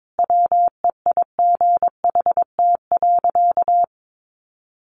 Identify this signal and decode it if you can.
Morse code, heard as a keyed tone, about 22 wpm, 700 Hz. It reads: WEIGHT.
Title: Sentences from Top 800 Words 22wpm